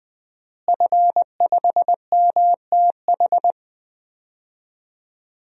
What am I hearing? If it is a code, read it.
Morse code, heard as a keyed tone, about 20 wpm, 700 Hz. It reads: F5MTH
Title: DXCC-100 Call Signs - 5 Characters - Repeated 2x - 20wpm